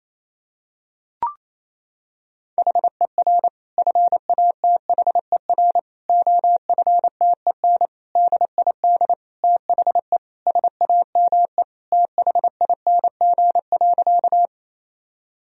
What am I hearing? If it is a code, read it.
Morse code, heard as a keyed tone, about 28 wpm, 700 Hz. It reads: HER FATHER OFTEN DID THE SAME THING.
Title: Sentences from Top 1000 Words - Review Part 1 - 28wpm